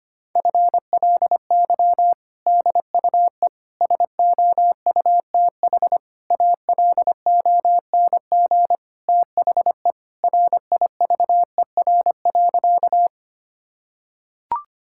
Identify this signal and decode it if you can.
Morse code, heard as a keyed tone, about 25 wpm, 700 Hz. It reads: FLY DUE SOUTH ALONG THE RIVER.
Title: Sentences from Top 1000 Words - Review Part 6 - 25wpm